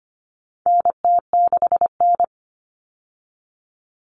Morse code, heard as a keyed tone, about 25 wpm, 700 Hz. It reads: NT6N
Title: All Cut Numbers - 4-Digit 25wpm